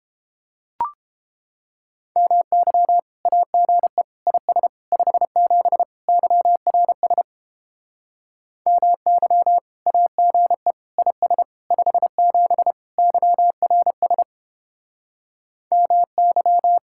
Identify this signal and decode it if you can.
Morse code, heard as a keyed tone, about 29 wpm, 700 Hz. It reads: MY AGE IS 57 YRS MY AGE IS 57 YRS MY